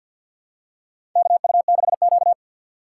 Morse code, heard as a keyed tone, about 52 wpm, 700 Hz. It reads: KU6Q
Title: USA Call Signs - 3x Character Spacing - 50wpm